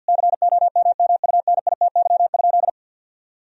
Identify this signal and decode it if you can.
Morse code, heard as a keyed tone, about 50 wpm, 700 Hz. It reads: COMMUNITY?
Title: Sentences from Top 1000 Words - Review Part 4 - 50wpm